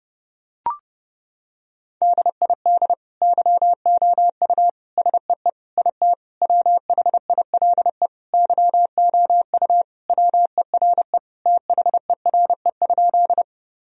Morse code, heard as a keyed tone, about 32 wpm, 700 Hz. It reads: DID YOU SEE IT WHILE YOU WERE THERE?
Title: Sentences from Top 1000 Words - Review Part 1 - 30wpm